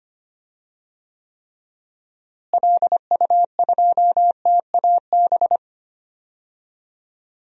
Morse code, heard as a keyed tone, about 25 wpm, 700 Hz. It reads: LU2TAB
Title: DXCC-100 Call Signs - 6 Characters - Repeated 2x - 25wpm